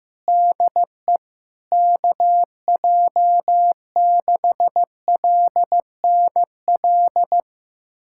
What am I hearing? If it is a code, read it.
Morse code, heard as a keyed tone, about 15 wpm, 700 Hz. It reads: DE KJ6LNL
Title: USA Callsign DE Callsign - 6 Characters - 15wpm